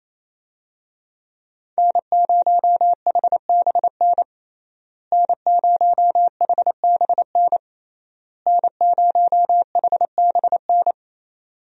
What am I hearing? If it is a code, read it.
Morse code, heard as a keyed tone, about 28 wpm, 700 Hz. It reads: N0HBN N0HBN N0HBN
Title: USA Call Signs - 5 Characters - Repeated 3x - 28wpm